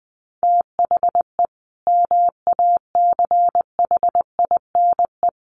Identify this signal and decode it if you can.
Morse code, heard as a keyed tone, about 20 wpm, 700 Hz. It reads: THE MACHINE
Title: A Sound Of Thunder 20wpm